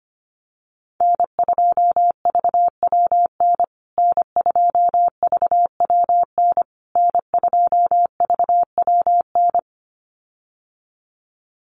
Morse code, heard as a keyed tone, about 25 wpm, 700 Hz. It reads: N2VWN N2VWN N2VWN
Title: USA Call Signs Repeated 3x 25wpm